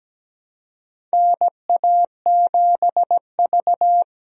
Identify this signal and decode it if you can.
Morse code, heard as a keyed tone, about 17 wpm, 700 Hz. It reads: NA7V
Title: USA Call Signs - 4 Characters - Repeated 2x - 17wpm